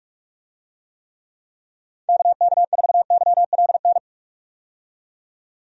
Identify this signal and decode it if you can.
Morse code, heard as a keyed tone, about 44 wpm, 700 Hz. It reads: KK4YLN
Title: USA Call Signs - 6 Characters - 45wpm